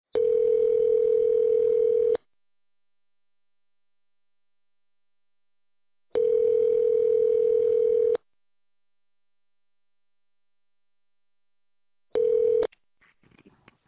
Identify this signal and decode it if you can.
Ringback tone (North America), 440 + 480 Hz.